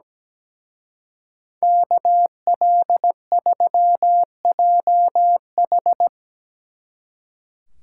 Morse code, heard as a keyed tone, about 19 wpm, 700 Hz. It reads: KL3JH
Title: Speed Racing - USA Call Signs 17wpm